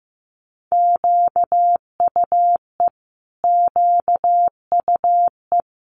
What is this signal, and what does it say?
Morse code, heard as a keyed tone, about 15 wpm, 700 Hz. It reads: QUE QUE
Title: Multi-letter Phonemes Repeated 3x 15wpm